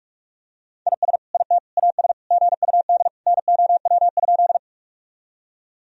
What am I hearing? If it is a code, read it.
Morse code, heard as a keyed tone, about 46 wpm, 700 Hz. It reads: IS IT AS GUD NOW?
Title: Sentences from Top 200 Words and Abbreviations 45wpm